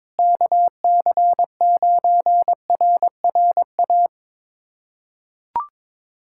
Morse code, heard as a keyed tone, about 21 wpm, 700 Hz. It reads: KC9RRA
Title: USA Callsign DE Callsign 22wpm